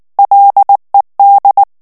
Morse code, heard as a keyed tone, about 19 wpm, 800 Hz. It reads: LED
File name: ANGLAIS texte clair 2 vitesse 18 mots minute